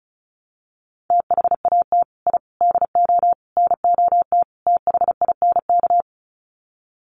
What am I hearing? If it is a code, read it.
Morse code, heard as a keyed tone, about 34 wpm, 700 Hz. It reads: THAT I DO NOT THINK
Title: Sets of 5 Words Encore 35wpm